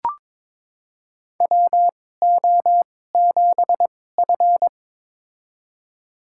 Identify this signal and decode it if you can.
Morse code, heard as a keyed tone, about 22 wpm, spaced as for 16 wpm, 700 Hz. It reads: WO7F
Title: USA Call Signs - 4 Characters - 2x Character Spacing - 22wpm